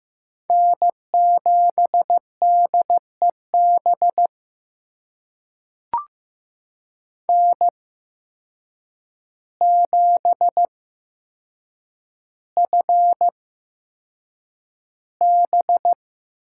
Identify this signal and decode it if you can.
Morse code, heard as a keyed tone, about 15 wpm, spaced as for 7 wpm, 700 Hz. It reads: N7DEB N7FB
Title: USA Call Signs - 5 Characters - 8x Character Spacing - 15wpm